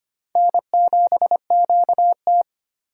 Morse code, heard as a keyed tone, about 25 wpm, 700 Hz. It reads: N7QT
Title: K1USN SST QSO Practice - 25wpm